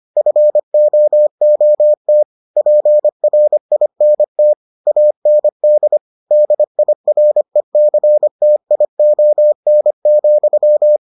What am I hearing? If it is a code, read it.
Morse code, heard as a keyed tone, about 25 wpm, 600 Hz. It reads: FOOT PRINT AND DIRECTION,